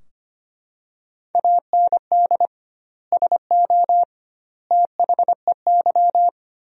Morse code, heard as a keyed tone, about 25 wpm, 700 Hz. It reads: AND SO THEY
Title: 2x Word-Spacing - Sentences from Top 400 Words - 25wpm